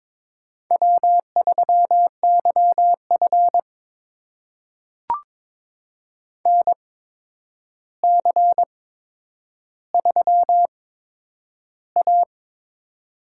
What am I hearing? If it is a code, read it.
Morse code, heard as a keyed tone, about 23 wpm, spaced as for 9 wpm, 700 Hz. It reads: W3YF NC3A
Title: USA Call Signs - 4 Characters - 8x Character Spacing - 22wpm